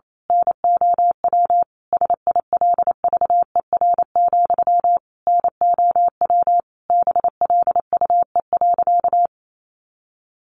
Morse code, heard as a keyed tone, about 28 wpm, 700 Hz. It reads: NOW SILVER, NOW BLUE.
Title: A Sound Of Thunder 28wpm